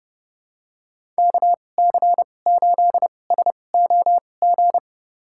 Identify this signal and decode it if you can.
Morse code, heard as a keyed tone, about 29 wpm, spaced as for 21 wpm, 700 Hz. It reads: KC8SOG
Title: USA Call Signs - 2x Character Spacing - 30wpm